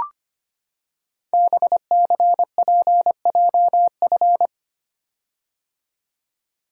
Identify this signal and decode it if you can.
Morse code, heard as a keyed tone, about 25 wpm, 700 Hz. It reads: BCPJF
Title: Code Groups - 5 Letters - 25wpm